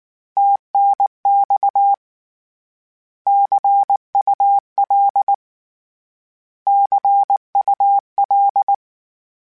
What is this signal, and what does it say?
Morse code, heard as a keyed tone, about 19 wpm, 800 Hz. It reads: TNX CUL CUL
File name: Abreviations / Abreviations mp3s / Abreviations@18